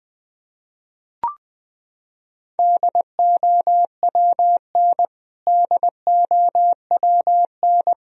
Morse code, heard as a keyed tone, about 21 wpm, 700 Hz. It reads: DOWN DOWN
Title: Top 100 Words Repeated 20wpm